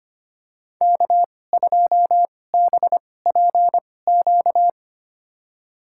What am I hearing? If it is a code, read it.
Morse code, heard as a keyed tone, about 25 wpm, spaced as for 18 wpm, 700 Hz. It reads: K2BPQ
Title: USA Call Signs - 5 Characters - 2x Character Spacing - 25wpm